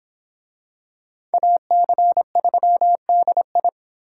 Morse code, heard as a keyed tone, about 26 wpm, 700 Hz. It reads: AC3DI